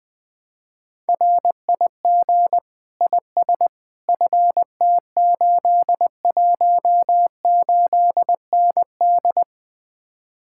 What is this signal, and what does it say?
Morse code, heard as a keyed tone, about 20 wpm, 700 Hz. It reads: RIG IS FT818ND
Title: Speed-Racing Common QSO Phrases Encore - 17wpm